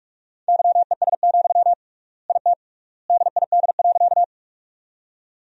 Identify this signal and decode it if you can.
Morse code, heard as a keyed tone, about 46 wpm, 700 Hz. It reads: YES, IT DID.
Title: 3x Word-Spacing - Sentences from Top 500 Words - 45wpm